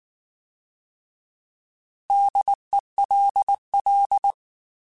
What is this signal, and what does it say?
Morse code, heard as a keyed tone, about 19 wpm, 800 Hz. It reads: DELL
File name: Alphabetical / Alphabetical mp3s / Ds @18